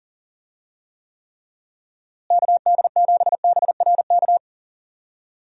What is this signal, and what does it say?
Morse code, heard as a keyed tone, about 40 wpm, 700 Hz. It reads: KD7BRK
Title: USA Call Signs - 6 Characters - Repeated 3x - 40wpm